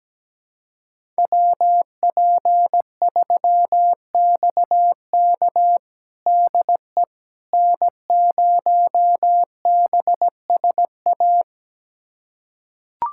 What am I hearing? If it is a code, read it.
Morse code, heard as a keyed tone, about 17 wpm, 700 Hz. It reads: WP3XK DE N0BSA
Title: USA Callsign DE Callsign - 5 Characters - 17wpm